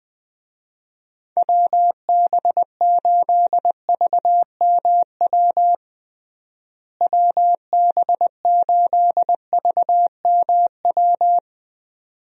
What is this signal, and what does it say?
Morse code, heard as a keyed tone, about 20 wpm, 700 Hz. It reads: WB8VMW WB8VMW
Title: USA Call Signs - 6 Characters - Repeated 2x - 20wpm